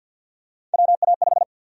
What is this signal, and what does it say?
Morse code, heard as a keyed tone, about 49 wpm, 700 Hz. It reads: WA5
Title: USA Call Sign Prefixes - Repeated 2x - 50wpm